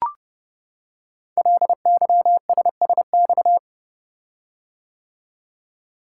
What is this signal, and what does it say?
Morse code, heard as a keyed tone, about 30 wpm, 700 Hz. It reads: LYSSX
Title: Code Groups - 5 Letters - 30wpm